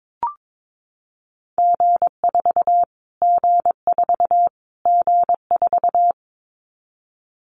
Morse code, heard as a keyed tone, about 22 wpm, 700 Hz. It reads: G4 G4 G4
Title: Callsign Prefixes for Top 25 DXCC - Repeated 3x - 22wpm